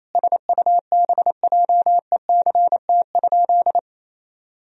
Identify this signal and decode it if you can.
Morse code, heard as a keyed tone, about 28 wpm, 700 Hz. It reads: SUBJECT?
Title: Sentences from Top 1000 Words - Review Part 4 - 28wpm